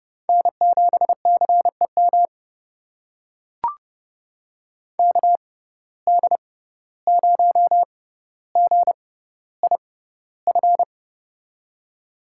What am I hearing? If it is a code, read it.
Morse code, heard as a keyed tone, about 30 wpm, spaced as for 15 wpm, 700 Hz. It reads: N7CEM KD0GIF
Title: USA Call Signs - 6x Character Spacing - 30wpm